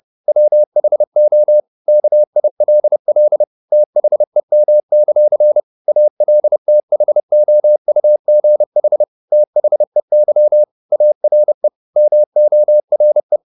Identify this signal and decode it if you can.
Morse code, heard as a keyed tone, about 30 wpm, 600 Hz. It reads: WHO KILL THEM; ALTHOUGH THEY ARE MORE
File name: the old man and the sea / the old man and the sea 1 30wpm